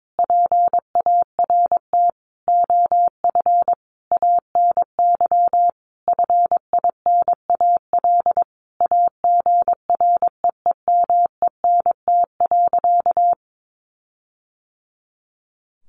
Morse code, heard as a keyed tone, about 22 wpm, 700 Hz. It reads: PART OF ANY FINAL AGREEMENT.
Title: Sentences from Top 1500 Words 22wpm